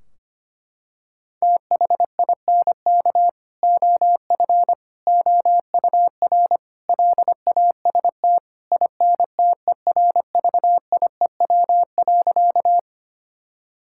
Morse code, heard as a keyed tone, about 25 wpm, 700 Hz. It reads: THINK OF OUR LAST INTERVIEW.
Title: Sentences from Top 1600 Words 25wpm